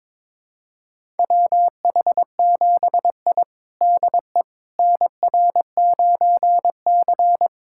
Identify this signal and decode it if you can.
Morse code, heard as a keyed tone, about 22 wpm, 700 Hz. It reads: WH7I DE NR9C
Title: USA Callsign DE Callsign - 4 Characters - 22wpm